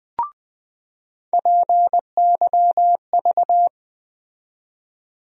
Morse code, heard as a keyed tone, about 20 wpm, 700 Hz. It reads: PYV
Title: USA Call Sign Postfixes - 3 Letters 20wpm